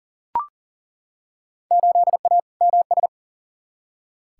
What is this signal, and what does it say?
Morse code, heard as a keyed tone, about 40 wpm, 700 Hz. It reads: 8A MS
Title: Field Day Exchanges - 40wpm